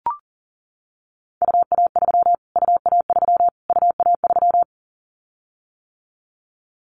Morse code, heard as a keyed tone, about 37 wpm, 700 Hz. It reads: UA3 UA3 UA3
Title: Callsign Prefixes for Top 25 DXCC - Repeated 3x - 40wpm